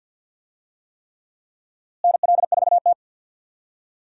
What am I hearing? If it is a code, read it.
Morse code, heard as a keyed tone, about 52 wpm, 700 Hz. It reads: NL4T
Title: USA Call Signs - 4 Characters - Repeated 3x - 50wpm